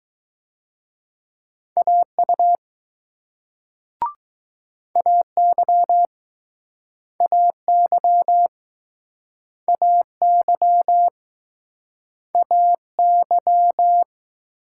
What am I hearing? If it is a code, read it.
Morse code, heard as a keyed tone, about 21 wpm, 700 Hz. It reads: AU AY AY AY AY